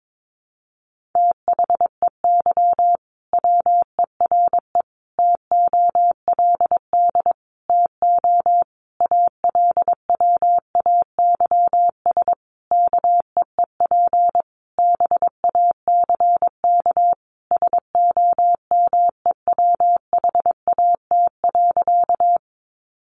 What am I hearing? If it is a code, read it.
Morse code, heard as a keyed tone, about 22 wpm, 700 Hz. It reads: THEY WERE TOLD TO ALWAYS KEEP BACK SOMEWHAT.
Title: Sentences from Top 1300 Words 22wpm